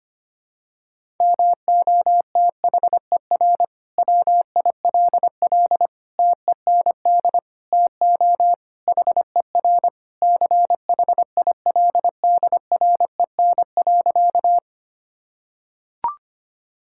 Morse code, heard as a keyed tone, about 25 wpm, 700 Hz. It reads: MOTHER WILL TEND TO HER CHILDREN.
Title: Sentences from Top 1000 Words - Review Part 5 - 25wpm